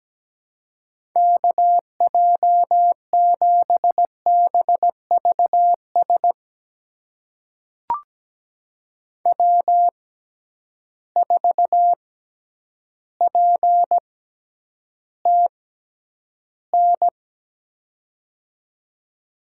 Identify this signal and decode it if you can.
Morse code, heard as a keyed tone, about 17 wpm, spaced as for 9 wpm, 700 Hz. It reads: KJ7BVS W4PTN
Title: USA Call Signs - 6x Character Spacing - 17wpm